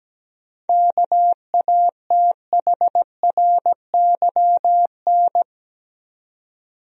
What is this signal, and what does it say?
Morse code, heard as a keyed tone, about 17 wpm, 700 Hz. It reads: KATHRYN
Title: USA Common Names - 17wpm